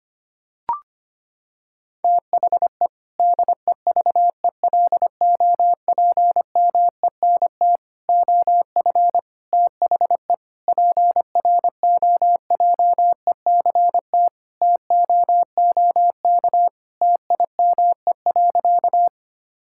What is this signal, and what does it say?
Morse code, heard as a keyed tone, about 25 wpm, 700 Hz. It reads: THE DEVELOPMENT OF THE PROJECT TOOK TIME.
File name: practice / 25wpm / Sentences with 3-Word Phrases - 25wpm